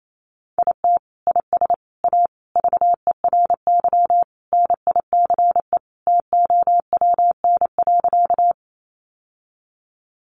Morse code, heard as a keyed tone, about 28 wpm, 700 Hz. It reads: IT IS A VERY NICE TOWN.
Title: Sentences from Top 1100 Words 28wpm